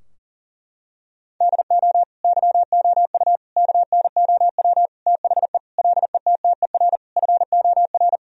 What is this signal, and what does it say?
Morse code, heard as a keyed tone, about 40 wpm, 700 Hz. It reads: DO YOU KNOW THE LETTER FOR
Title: Sentences from Top 1000 Words - Review Part 2 - 40wpm